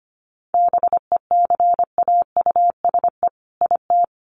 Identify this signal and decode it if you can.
Morse code, heard as a keyed tone, about 25 wpm, 700 Hz. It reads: BECAUSE IT
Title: Sets of 2 Words 25wpm